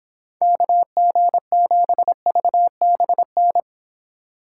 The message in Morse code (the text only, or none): KG7VBN